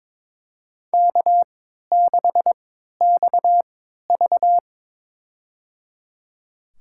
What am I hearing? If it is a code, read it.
Morse code, heard as a keyed tone, about 21 wpm, spaced as for 12 wpm, 690 Hz. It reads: K6XV
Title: USA Call Signs - 4 Characters - 3x Character Spacing - 22wpm